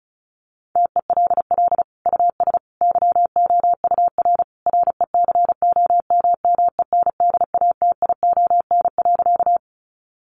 Morse code, heard as a keyed tone, about 34 wpm, 700 Hz. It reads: TELL US YOUR RECOMMENDATION.
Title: Sentences from Top 1400 Words 35wpm